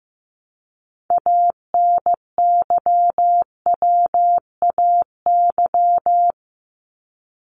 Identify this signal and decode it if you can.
Morse code, heard as a keyed tone, about 15 wpm, 700 Hz. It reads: ANYWAY
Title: Top 1501-1750 Words 15wpm